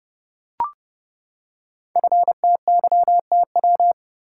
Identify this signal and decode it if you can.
Morse code, heard as a keyed tone, about 30 wpm, 700 Hz. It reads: FTYTW
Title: Code Groups - 5 Letters - 30wpm